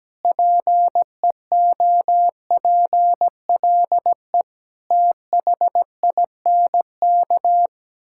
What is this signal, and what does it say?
Morse code, heard as a keyed tone, about 17 wpm, 700 Hz. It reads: PEOPLE THINK
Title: Sentences from Top 1000 Words - Review Part 5 - 17wpm